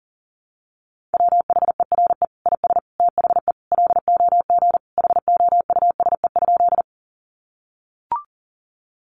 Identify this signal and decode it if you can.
Morse code, heard as a keyed tone, about 40 wpm, 700 Hz. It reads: WHERE IS THE LOG HOUSE?
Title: Sentences from Top 1400 Words 40wpm